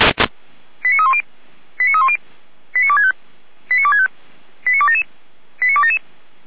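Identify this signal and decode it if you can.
Five-tone selective call (ZVEI), heard as quick runs of five tones, tones of about 70 ms. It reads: 89310 89310 89265 89265 89277 89277
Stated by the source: ZVEI five-tone call received on SDR.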